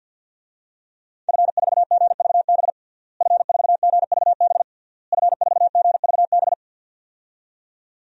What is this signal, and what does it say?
Morse code, heard as a keyed tone, about 52 wpm, 700 Hz. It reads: F4GVB F4GVB F4GVB
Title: DXCC-100 Call Signs - 5 Characters - Repeated 3x - 50wpm